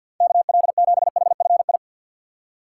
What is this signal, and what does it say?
Morse code, heard as a keyed tone, about 51 wpm, 700 Hz. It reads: KL7HFI
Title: K1USN SST QSO Practice - 50wpm